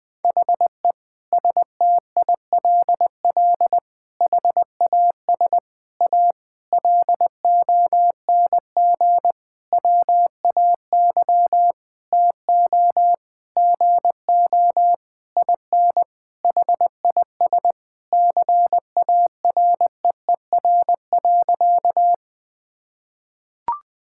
Morse code, heard as a keyed tone, about 20 wpm, 700 Hz. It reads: HE STILL HAS A LONG WAY TO GO IN HIS CAREER.
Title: Sentences with 7-Word Phrases - 20wpm